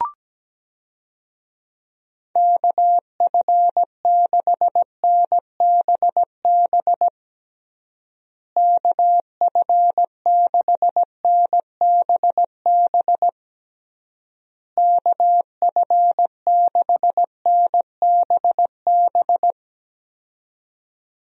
Morse code, heard as a keyed tone, about 17 wpm, 700 Hz. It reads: KF6NBB KF6NBB KF6NBB
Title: USA Call Signs - 6 Characters - Repeated 3x - 17wpm